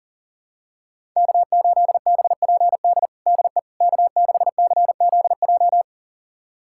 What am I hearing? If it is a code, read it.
Morse code, heard as a keyed tone, about 37 wpm, 700 Hz. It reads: K8BPD DE K6CZJ
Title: USA Callsign DE Callsign - 5 Characters - 40wpm